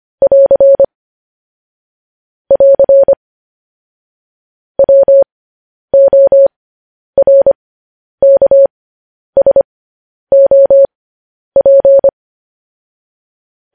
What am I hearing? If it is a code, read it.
Morse code, heard as a keyed tone, about 25 wpm, spaced as for 10 wpm, 550 Hz. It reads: <AR> <AR> WORKSOP
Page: News Headlines in Morse Code at 10 WPM